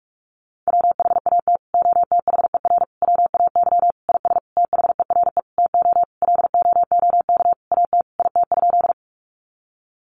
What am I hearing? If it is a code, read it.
Morse code, heard as a keyed tone, about 44 wpm, 700 Hz. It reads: WHAT OTHER WAY IS THERE TO LOOK AT IT?
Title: Sentences from Top 100 Words 45wpm